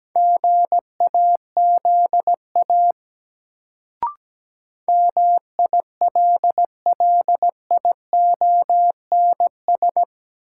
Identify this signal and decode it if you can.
Morse code, heard as a keyed tone, about 17 wpm, 700 Hz. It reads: GAZA MILLIONS
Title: Sets of 5 Words - New Word Drill - 17wpm